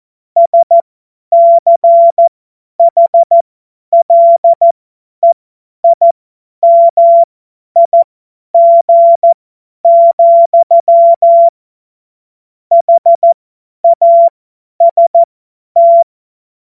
Morse code, heard as a keyed tone, about 14 wpm, spaced as for 10 wpm, 680 Hz. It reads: SCHLEIMIG, HAST
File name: Morsekurs / Witze / Witz 25. Oktober 2022 50 F70